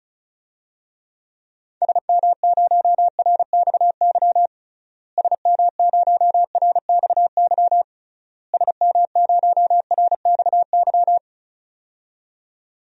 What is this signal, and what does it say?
Morse code, heard as a keyed tone, about 36 wpm, 700 Hz. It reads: SM0RXY SM0RXY SM0RXY
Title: DXCC-100 Call Signs - 6 Characters - Repeated 3x - 35wpm